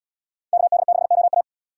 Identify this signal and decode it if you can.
Morse code, heard as a keyed tone, about 68 wpm, 700 Hz. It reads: KD6OA